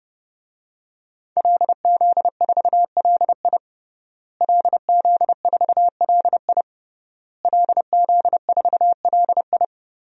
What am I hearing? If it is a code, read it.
Morse code, heard as a keyed tone, about 32 wpm, 700 Hz. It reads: LZ4LI LZ4LI LZ4LI
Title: DXCC-100 Call Signs - 5 Characters - Repeated 3x - 30wpm